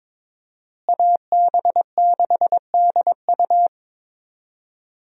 Morse code, heard as a keyed tone, about 22 wpm, 700 Hz. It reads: AB6DU
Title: USA Call Signs - 5 Characters - Repeated 2x - 22wpm